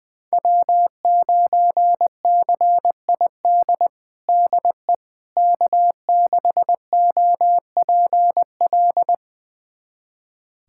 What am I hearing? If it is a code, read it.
Morse code, heard as a keyed tone, about 20 wpm, 700 Hz. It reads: W9CID DE K6OPL
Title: USA Callsign DE Callsign - 5 Characters - 20wpm